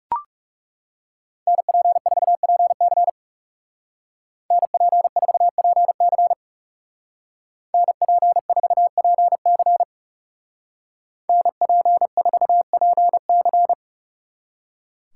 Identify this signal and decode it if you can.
Morse code, heard as a keyed tone, about 39 wpm, 700 Hz. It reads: NP4PC NP4PC NP4PC NP4PC